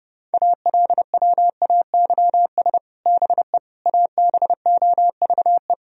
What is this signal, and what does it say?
Morse code, heard as a keyed tone, about 32 wpm, 700 Hz. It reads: ALWAYS BE ABOVE
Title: Sentences from Top 1000 Words - Review Part 2 - 30wpm